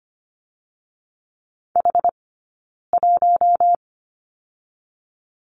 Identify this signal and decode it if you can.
Morse code, heard as a keyed tone, about 24 wpm, 700 Hz. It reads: H 1
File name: Lesson 15d - Number 5 - Two Characters 25wpm